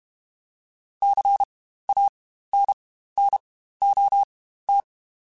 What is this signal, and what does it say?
Morse code, heard as a keyed tone, about 32 wpm, spaced as for 15 wpm, 800 Hz. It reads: CANNOT
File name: Quotes Spaced / Quotes Spaced-mp3s / Advice sp @30